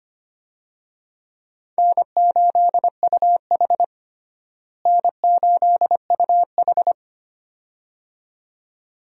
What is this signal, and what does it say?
Morse code, heard as a keyed tone, about 25 wpm, 700 Hz. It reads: N8UH N8UH
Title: USA Call Signs - 4 Characters - Repeated 2x - 25wpm